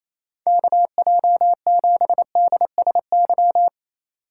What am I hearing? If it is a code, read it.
Morse code, heard as a keyed tone, about 27 wpm, 700 Hz. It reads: KJ7DSY
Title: USA Call Signs - 6 Characters - 3x Character Spacing - 28wpm